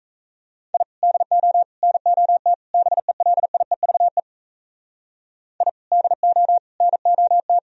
Morse code, heard as a keyed tone, about 37 wpm, 700 Hz. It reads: I DO NOT BELIEVE I DO NOT